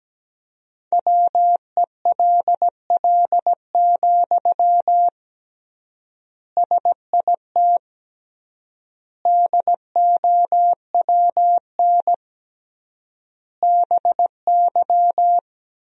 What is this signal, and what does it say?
Morse code, heard as a keyed tone, about 17 wpm, 700 Hz. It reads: WELL, SIT DOWN BY
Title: 3x Word-Spacing - Sentences from Top 500 Words - 17wpm